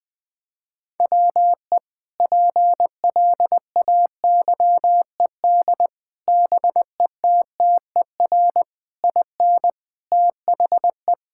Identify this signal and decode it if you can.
Morse code, heard as a keyed tone, about 20 wpm, 700 Hz. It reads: WE PLAYED BETTER IN THE
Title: Sentences with 5-Word Phrases - 20wpm